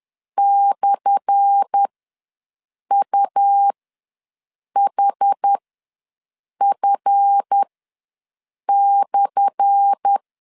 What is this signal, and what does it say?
Morse code, heard as a keyed tone, about 10 wpm, spaced as for 6 wpm, 800 Hz. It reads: /UHF/
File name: LEZIONI / 10 SIGNS / 10A SLASH ws 10wpm 4121